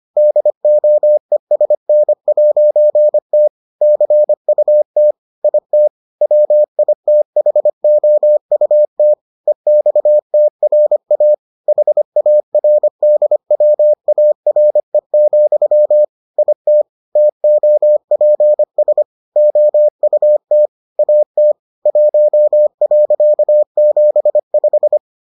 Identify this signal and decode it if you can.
Morse code, heard as a keyed tone, about 25 wpm, 600 Hz. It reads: DOESN'T CUT IT WITHOUT EXTRA HARDWARE, IT TOPS OUT AT 1.75